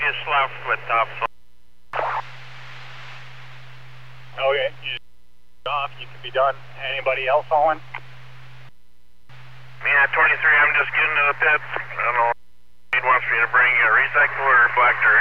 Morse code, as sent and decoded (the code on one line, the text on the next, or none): none